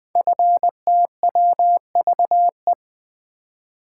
Morse code, heard as a keyed tone, about 20 wpm, 700 Hz. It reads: FTWVE
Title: Code Groups - 5 Letters - 20wpm